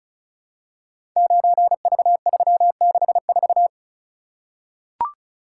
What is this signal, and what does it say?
Morse code, heard as a keyed tone, about 34 wpm, 700 Hz. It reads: 9V364
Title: Code Groups - 5 Letter:Numbers - 35wpm